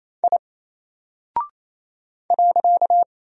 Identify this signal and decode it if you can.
Morse code, heard as a keyed tone, about 28 wpm, 700 Hz. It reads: I .
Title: Single Character (Letter:Number:Punctuation:Prosign) Alt 28wpm